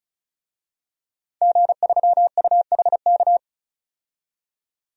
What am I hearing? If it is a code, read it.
Morse code, heard as a keyed tone, about 34 wpm, 700 Hz. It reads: G3UHK
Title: DXCC-100 Call Signs - 5 Characters - 35wpm